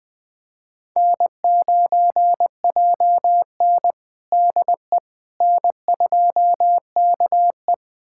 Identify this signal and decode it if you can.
Morse code, heard as a keyed tone, about 20 wpm, 700 Hz. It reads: N9JN DE N2KE
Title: USA Callsign DE Callsign - 4 Characters - 20wpm